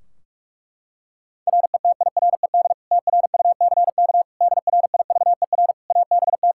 Morse code, heard as a keyed tone, about 46 wpm, 700 Hz. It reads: RETIRED TRUCK DRIVER ABT
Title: Practice QSO - Exchange 3 - More Info - 45wpm